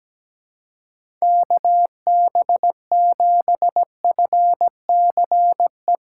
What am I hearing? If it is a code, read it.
Morse code, heard as a keyed tone, about 17 wpm, 700 Hz. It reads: KB7FCE
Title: USA Callsign DE Callsign - 6 Characters - 17wpm